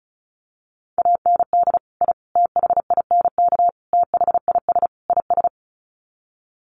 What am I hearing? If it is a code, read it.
Morse code, heard as a keyed tone, about 34 wpm, 700 Hz. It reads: AND I THINK THIS IS
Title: Sets of 5 Words Encore 35wpm